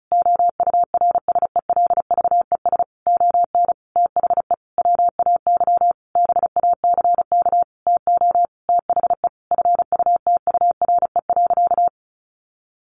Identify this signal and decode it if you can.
Morse code, heard as a keyed tone, about 34 wpm, 700 Hz. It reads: OURSELVES ON THE WAY BACK TO THE FUTURE.
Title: A Sound Of Thunder 35wpm